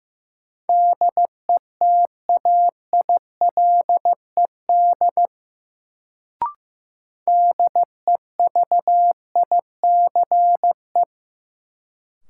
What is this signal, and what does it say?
Morse code, heard as a keyed tone, about 15 wpm, 700 Hz. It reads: DETAILED DEVICE
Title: Sentences from Top 1500 Words - New Word Drill Alt - 15wpm